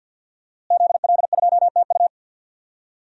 Morse code, heard as a keyed tone, about 47 wpm, 700 Hz. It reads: ZL2TU
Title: DXCC-100 Call Signs - 5 Characters - Repeated 3x - 50wpm